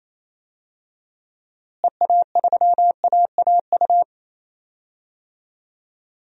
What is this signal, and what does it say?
Morse code, heard as a keyed tone, about 28 wpm, 700 Hz. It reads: EA3AAU
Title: DXCC-100 Call Signs - 6 Characters - Repeated 3x - 28wpm